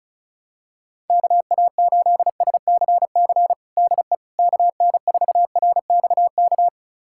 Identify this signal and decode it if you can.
Morse code, heard as a keyed tone, about 34 wpm, 700 Hz. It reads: KA8SCC DE KN4RXK
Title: USA Callsign DE Callsign 35wpm